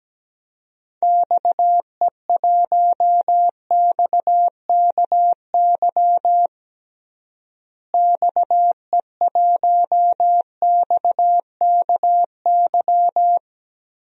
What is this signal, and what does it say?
Morse code, heard as a keyed tone, about 17 wpm, 700 Hz. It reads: XE1XKY XE1XKY
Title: DXCC-100 Call Signs - 6 Characters - Repeated 3x - 17wpm